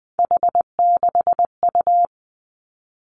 Morse code, heard as a keyed tone, about 20 wpm, 700 Hz. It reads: H6U
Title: Three Challenging Letter-Number Combinations 20wpm